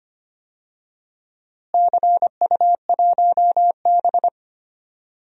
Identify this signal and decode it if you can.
Morse code, heard as a keyed tone, about 25 wpm, 700 Hz. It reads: CU1B
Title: DXCC-100 Call Signs - 4 Characters - Repeated 3x - 25wpm